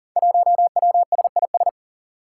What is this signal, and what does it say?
Morse code, heard as a keyed tone, about 42 wpm, 700 Hz. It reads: 1WSIS
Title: Five Challenging Letter-Number Combinations 40wpm